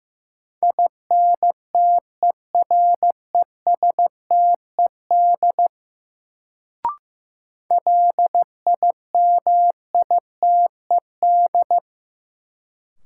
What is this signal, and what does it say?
Morse code, heard as a keyed tone, about 16 wpm, 700 Hz. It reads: INTERESTED LIMITED
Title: Sentences from Top 900 Words - New Word Drill - 15wpm